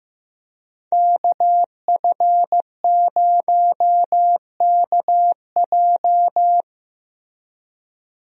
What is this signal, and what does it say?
Morse code, heard as a keyed tone, about 15 wpm, 700 Hz. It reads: KF0KJ
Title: USA Call Signs - 5 Characters - Repeated 2x - 15wpm